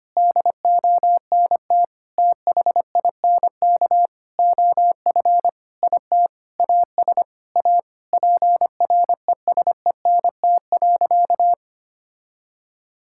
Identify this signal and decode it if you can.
Morse code, heard as a keyed tone, about 25 wpm, 700 Hz. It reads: DONT THINK OF IT AS A PRESENT.
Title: Sentences from Top 1000 Words - Review Part 5 - 25wpm